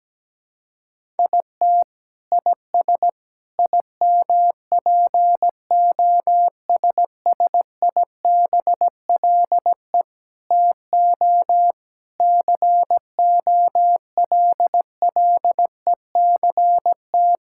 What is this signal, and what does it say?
Morse code, heard as a keyed tone, about 17 wpm, 700 Hz. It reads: IT IS IMPOSSIBLE TO COLLECT
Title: Sentences from Top 1600 Words 17wpm